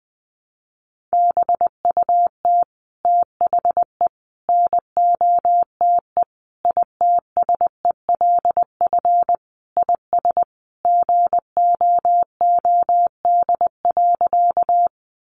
Morse code, heard as a keyed tone, about 20 wpm, 700 Hz. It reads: BUT THE NOTE ITSELF IS GOOD.
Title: Sentences from Top 800 Words 20wpm